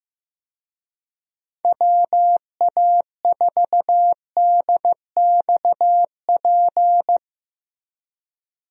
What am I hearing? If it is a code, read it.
Morse code, heard as a keyed tone, about 15 wpm, 700 Hz. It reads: WA4DXP
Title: USA Call Signs - 6 Characters - Repeated 3x - 15wpm